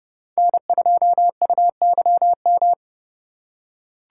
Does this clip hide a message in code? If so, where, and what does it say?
Yes: Morse code from 0.4 to 2.7 s, heard as a keyed tone, about 30 wpm, 700 Hz. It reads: N2UYM